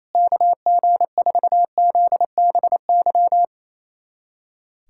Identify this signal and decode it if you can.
Morse code, heard as a keyed tone, about 28 wpm, 700 Hz. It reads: KG4ZBY
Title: USA Callsign DE Callsign 28wpm